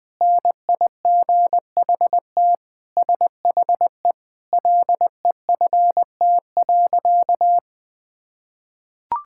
Morse code, heard as a keyed tone, about 20 wpm, 700 Hz. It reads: NIGHT SHE LEFT.
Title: Sentences with 3-Word Phrases - 20wpm